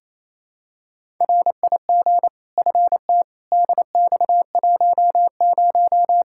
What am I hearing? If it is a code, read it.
Morse code, heard as a keyed tone, about 28 wpm, 700 Hz. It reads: RIG FT DX10
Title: Practice QSO - Exchange 2 - Info - 28wpm